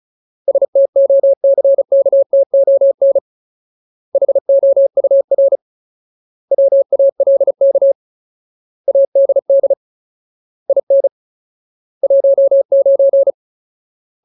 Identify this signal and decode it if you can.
Morse code, heard as a keyed tone, about 34 wpm, 550 Hz. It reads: STOCKTON HOUR WALK ADD IN 19
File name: Ultimate Word List / 35WPM Ultimate